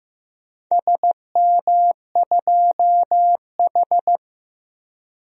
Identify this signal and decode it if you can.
Morse code, heard as a keyed tone, about 15 wpm, 700 Hz. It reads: SM2H
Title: DXCC-100 Call Signs - 4 Characters - Repeated 3x - 15wpm